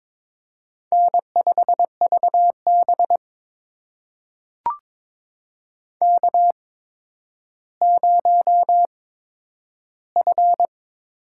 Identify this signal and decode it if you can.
Morse code, heard as a keyed tone, about 22 wpm, spaced as for 11 wpm, 700 Hz. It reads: N5VB K0F
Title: USA Call Signs - 4 Characters - 8x Character Spacing - 22wpm